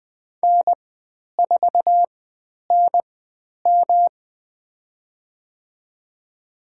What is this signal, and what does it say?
Morse code, heard as a keyed tone, about 20 wpm, spaced as for 10 wpm, 700 Hz. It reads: N4NM
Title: 109 Lesson 18g - Letter m - Call Signs 20wpm